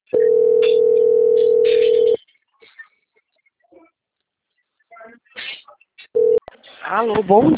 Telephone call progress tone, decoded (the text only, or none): ringback tone